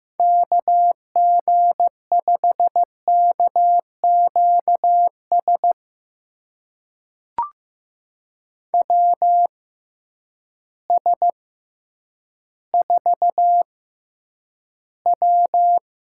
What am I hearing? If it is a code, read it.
Morse code, heard as a keyed tone, about 15 wpm, spaced as for 9 wpm, 700 Hz. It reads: KG5KQS WS4W
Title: USA Call Signs - 6x Character Spacing - 15wpm